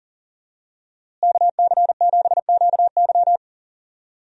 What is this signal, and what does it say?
Morse code, heard as a keyed tone, about 40 wpm, 700 Hz. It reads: KC7QY
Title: USA Call Signs - 5 Characters - Repeated 2x - 40wpm